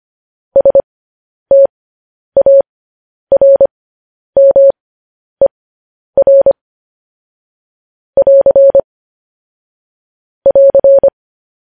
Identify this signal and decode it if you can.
Morse code, heard as a keyed tone, about 25 wpm, spaced as for 10 wpm, 550 Hz. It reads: STARMER <AR> <AR>
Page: News Headlines in Morse Code at 10 WPM